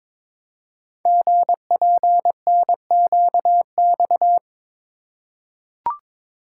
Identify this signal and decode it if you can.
Morse code, heard as a keyed tone, about 22 wpm, 700 Hz. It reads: GPNQX
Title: Code Groups - 5 Letter:Numbers - 22wpm